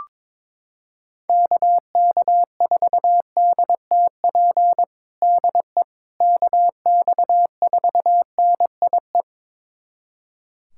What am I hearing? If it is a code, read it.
Morse code, heard as a keyed tone, about 22 wpm, 700 Hz. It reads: KK4DTP DE KX4NIE
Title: USA Callsign DE Callsign - 6 Characters - 22wpm